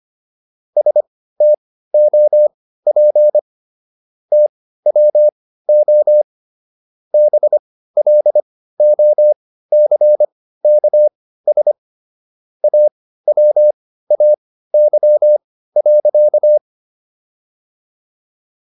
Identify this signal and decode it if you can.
Morse code, heard as a keyed tone, about 25 wpm, spaced as for 15 wpm, 600 Hz. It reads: STOP TWO BLOCKS AWAY.